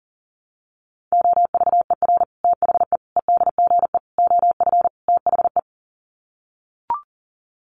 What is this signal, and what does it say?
Morse code, heard as a keyed tone, about 40 wpm, 700 Hz. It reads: OVER THE EDGE OF THE
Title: Sets of 5 Words Encore2 40wpm